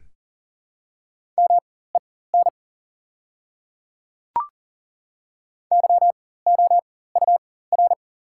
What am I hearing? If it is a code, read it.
Morse code, heard as a keyed tone, about 40 wpm, spaced as for 19 wpm, 700 Hz. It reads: MEN YOUR
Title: 4x Character Spacing - Top 300 Words - 40wpm